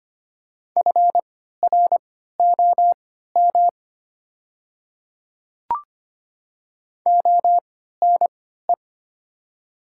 Morse code, heard as a keyed tone, about 25 wpm, spaced as for 14 wpm, 700 Hz. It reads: FROM ONE
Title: 3x Character Spacing - Top 100 Words - 25wpm